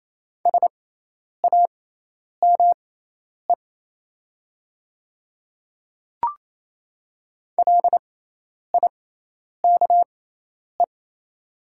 Morse code, heard as a keyed tone, about 28 wpm, spaced as for 10 wpm, 700 Hz. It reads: SAME LIKE
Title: 6x Character Spacing - Top 200 Words - 28wpm